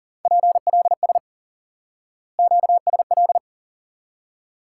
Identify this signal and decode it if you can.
Morse code, heard as a keyed tone, about 40 wpm, 700 Hz. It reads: PLS QSL
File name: IQE Course 3 - Lesson 89 - YAGI - Two Elements - 40wpm